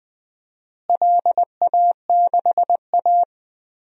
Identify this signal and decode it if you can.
Morse code, heard as a keyed tone, about 20 wpm, 700 Hz. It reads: LA6A
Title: DXCC-100 Call Signs - 4 Characters - Repeated 3x - 20wpm